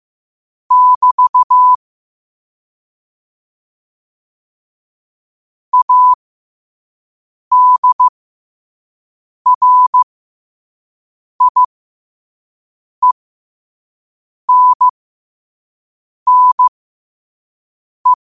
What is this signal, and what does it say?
Morse code, heard as a keyed tone, about 15 wpm, spaced as for 5 wpm, 1000 Hz. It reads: <BT> ADRIENNE
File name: qotdfiles / 5wpm